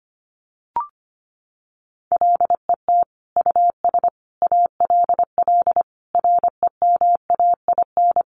Morse code, heard as a keyed tone, about 25 wpm, 700 Hz. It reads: LET US ALL REMAIN
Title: Sentences from Top 1200 Words 25wpm